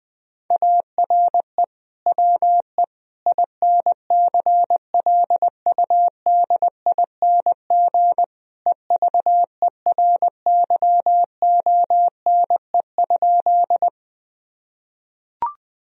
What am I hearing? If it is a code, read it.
Morse code, heard as a keyed tone, about 19 wpm, 700 Hz. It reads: ARE WE INCLUDING EVERYONE?
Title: Sentences from Top 1000 Words - Review Part 4 - 20wpm